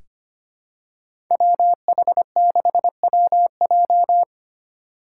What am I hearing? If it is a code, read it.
Morse code, heard as a keyed tone, about 25 wpm, 700 Hz. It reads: WH6WJ
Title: USA Call Signs - 5 Characters - 4x Character Spacing - 25wpm